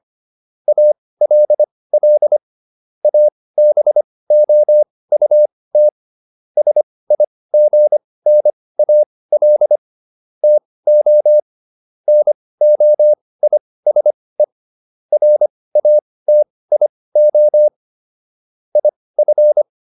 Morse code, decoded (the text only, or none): ALL ABOUT SIGNAL TO NOISE RATIO IF